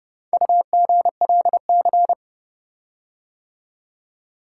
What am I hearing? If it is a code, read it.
Morse code, heard as a keyed tone, about 30 wpm, 700 Hz. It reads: UGLC